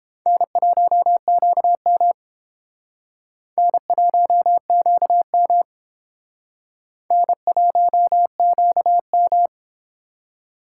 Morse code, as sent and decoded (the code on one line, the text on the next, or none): -. .---- --.- -- / -. .---- --.- -- / -. .---- --.- --
N1QM N1QM N1QM